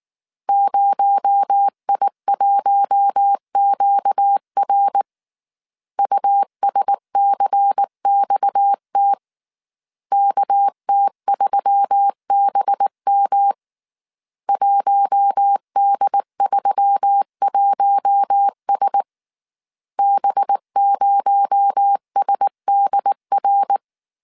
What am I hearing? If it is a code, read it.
Morse code, heard as a keyed tone, about 19 wpm, 800 Hz. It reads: 0I1QR USCXT KT3BM 1D31S B0SDR
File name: CW / DET / 1 6 1500C S A 18wpm 2255